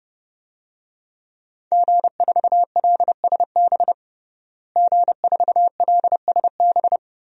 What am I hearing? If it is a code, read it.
Morse code, heard as a keyed tone, about 32 wpm, 700 Hz. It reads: G4LSB G4LSB
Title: DXCC-100 Call Signs - 5 Characters - Repeated 3x - 30wpm